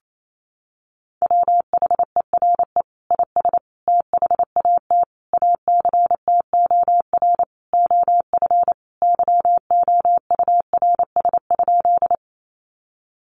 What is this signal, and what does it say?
Morse code, heard as a keyed tone, about 28 wpm, 700 Hz. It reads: WHERE IS THAT ACTOR OF YOURS?
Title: Sentences from Top 1500 Words 28wpm